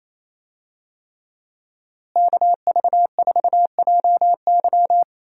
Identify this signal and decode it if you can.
Morse code, heard as a keyed tone, about 28 wpm, 700 Hz. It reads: KV4JY
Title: USA Call Signs - 5 Characters - Repeated 3x - 28wpm